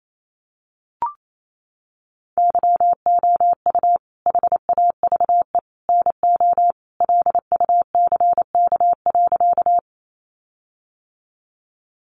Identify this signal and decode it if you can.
Morse code, heard as a keyed tone, about 28 wpm, 700 Hz. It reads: YOU HAVE NO LUCK.
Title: Sentences from Top 1500 Words 28wpm